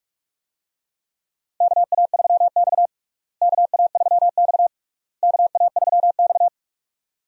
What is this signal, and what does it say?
Morse code, heard as a keyed tone, about 46 wpm, 700 Hz. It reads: KA3X KA3X KA3X
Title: USA Call Signs - 4 Characters - Repeated 3x - 45wpm